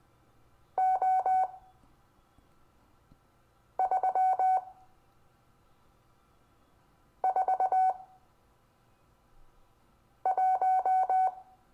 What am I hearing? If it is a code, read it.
Morse code, heard as a keyed tone, about 20 wpm, spaced as for 3 wpm, 750 Hz. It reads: O341
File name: cw / Lesson25 6 X